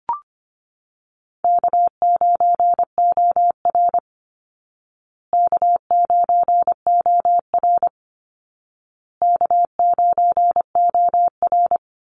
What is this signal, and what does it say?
Morse code, heard as a keyed tone, about 25 wpm, 700 Hz. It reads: K9OR K9OR K9OR
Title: Easy Call Signs Repeated 3x - 25wpm